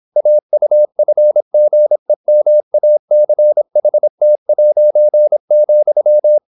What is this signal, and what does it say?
Morse code, heard as a keyed tone, about 26 wpm, 600 Hz. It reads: AUFGEMACHT',